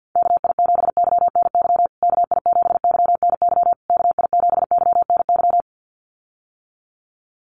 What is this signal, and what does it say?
Morse code, heard as a keyed tone, about 50 wpm, 700 Hz. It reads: KI7YNY KI7YNY KI7YNY
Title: USA Call Signs Repeated 3x 50wpm